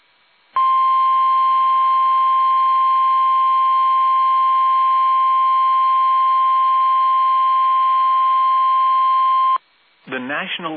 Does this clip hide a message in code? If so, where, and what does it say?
Yes: Emergency Alert System (SAME) tones (the 1050 Hz attention tone) from 0.5 to 9.6 s, with speech.